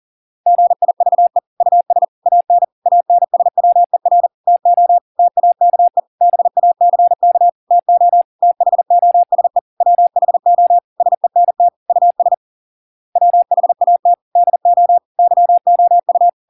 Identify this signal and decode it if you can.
Morse code, heard as a keyed tone, about 40 wpm, 700 Hz. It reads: GIVE US AN ANSWER TO TAKE BACK TO THOSE WHO SENT US WHAT DO YOU